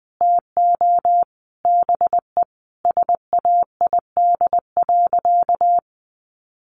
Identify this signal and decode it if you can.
Morse code, heard as a keyed tone, about 20 wpm, 700 Hz. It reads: TO BE SAID.
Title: Sentences from Top 500 Words 20wpm